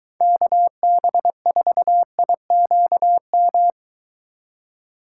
Morse code, heard as a keyed tone, about 22 wpm, 700 Hz. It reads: KB4IQM